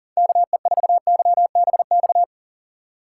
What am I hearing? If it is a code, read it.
Morse code, heard as a keyed tone, about 40 wpm, 700 Hz. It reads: KE4YBX
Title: USA Call Signs - 6 Characters - 3x Character Spacing - 40wpm